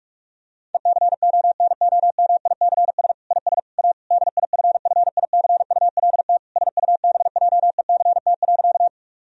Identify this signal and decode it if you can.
Morse code, heard as a keyed tone, about 46 wpm, 700 Hz. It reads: ECONOMICS IS A DIFFICULT SUBJECT.
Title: Sentences from Top 1000 Words - Review Part 3 - 45wpm